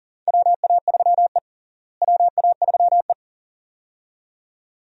Morse code, heard as a keyed tone, about 40 wpm, 700 Hz. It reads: WA3E WA3E
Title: USA Call Signs - 4 Characters - Repeated 2x - 40wpm